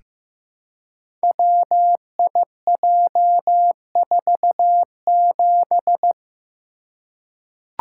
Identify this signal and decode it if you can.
Morse code, heard as a keyed tone, about 15 wpm, 700 Hz. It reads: WIJ47
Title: Code Groups - 5 Letter:Numbers - 15wpm